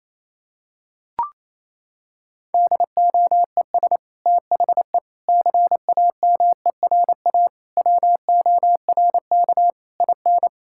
Morse code, heard as a keyed tone, about 27 wpm, 700 Hz. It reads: DOES THE CAMERA WORK IN
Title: Sentences from Top 1000 Words - Review Part 5 - 28wpm